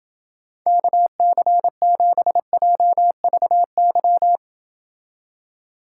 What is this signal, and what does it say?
Morse code, heard as a keyed tone, about 31 wpm, 700 Hz. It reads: KC7JVY